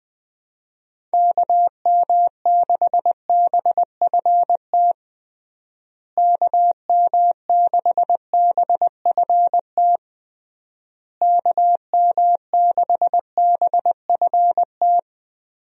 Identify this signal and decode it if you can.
Morse code, heard as a keyed tone, about 20 wpm, 700 Hz. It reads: KM6BFT KM6BFT KM6BFT
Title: USA Call Signs - 6 Characters - Repeated 3x - 20wpm